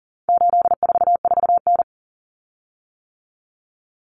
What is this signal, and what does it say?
Morse code, heard as a keyed tone, about 40 wpm, 700 Hz. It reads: 844N
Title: Common Cut Numbers - 4-Digit 40wpm